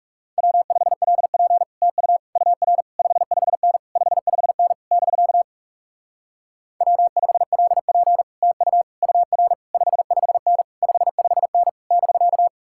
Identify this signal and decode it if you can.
Morse code, heard as a keyed tone, about 40 wpm, 700 Hz. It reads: W5LP TU UR 55N 55N <BK> W5LP TU UR 55N 55N <BK>